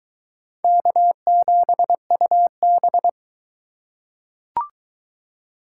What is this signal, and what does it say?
Morse code, heard as a keyed tone, about 22 wpm, 700 Hz. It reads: K7UB